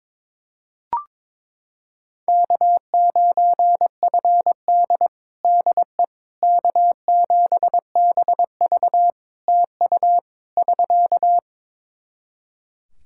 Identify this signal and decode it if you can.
Morse code, heard as a keyed tone, about 21 wpm, 700 Hz. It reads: K9FD DE K7BV TU <SK>
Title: Practice QSO - Exchange 4 - Ending - 22wpm